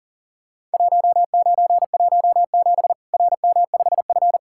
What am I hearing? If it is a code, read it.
Morse code, heard as a keyed tone, about 40 wpm, 700 Hz. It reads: 1917 RM5F
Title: Worked All Europe DX Contest - QTC Exchange - 40wpm